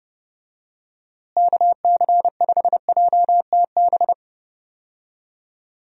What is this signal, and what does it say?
Morse code, heard as a keyed tone, about 32 wpm, 700 Hz. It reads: KC5JTB
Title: USA Call Signs - 6 Characters - Repeated 3x - 30wpm